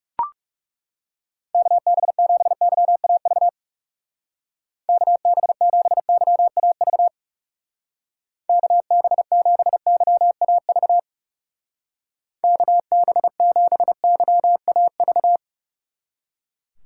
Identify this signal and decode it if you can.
Morse code, heard as a keyed tone, about 38 wpm, 700 Hz. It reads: KB7YAV KB7YAV KB7YAV KB7YAV